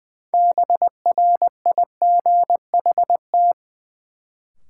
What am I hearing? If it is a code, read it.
Morse code, heard as a keyed tone, about 20 wpm, 700 Hz. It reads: BRIGHT
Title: Sentences from Top 1000 Words - New Word Drill - 20wpm